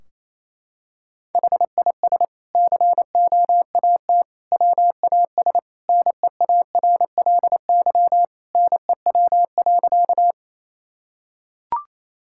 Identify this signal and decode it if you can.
Morse code, heard as a keyed tone, about 28 wpm, 700 Hz. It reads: HIS COAT WAS NEARLY NEW.
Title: Sentences from Top 1600 Words 28wpm